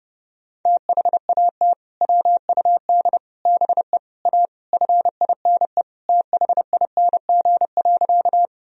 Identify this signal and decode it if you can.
Morse code, heard as a keyed tone, about 30 wpm, 700 Hz. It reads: THAT WUD BE A FINE THING.
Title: Sentences from Top 500 Words and Abbreviations 30wpm